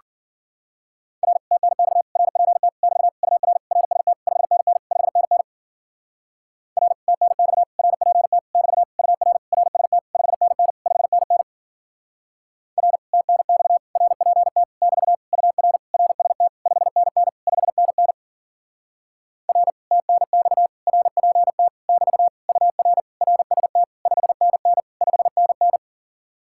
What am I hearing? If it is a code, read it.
Morse code, heard as a keyed tone, about 50 wpm, 700 Hz. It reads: R TNX RPT <BT> UR RST 5NN 5NN R TNX RPT <BT> UR RST 5NN 5NN R TNX RPT <BT> UR RST 5NN 5NN R TNX RPT <BT> UR RST 5NN 5NN
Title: Speed-Racing Common QSO Phrases - 40wpm